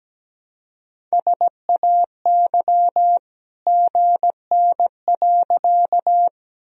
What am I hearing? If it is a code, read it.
Morse code, heard as a keyed tone, about 17 wpm, 700 Hz. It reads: SAY GN.
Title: Sentences from Top 300 Words and Abbreviations 17wpm